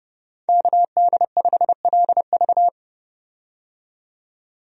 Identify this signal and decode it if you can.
Morse code, heard as a keyed tone, about 30 wpm, 700 Hz. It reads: KD5LV
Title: USA Call Signs - 5 Characters - 30wpm